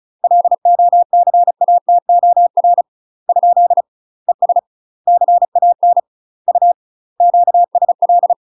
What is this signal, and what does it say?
Morse code, heard as a keyed tone, about 36 wpm, 700 Hz. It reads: LOCATOR ? ES CAN U QSL
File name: UK MASTER QSOs TRAINING / 35WPM / UK MASTER QSOs 0007